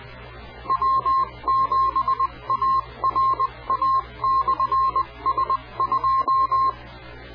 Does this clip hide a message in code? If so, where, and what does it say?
Yes: Morse code from 0.7 to 6.7 s, heard as a keyed tone, about 20 wpm, 1060 Hz. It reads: W7ARA/S2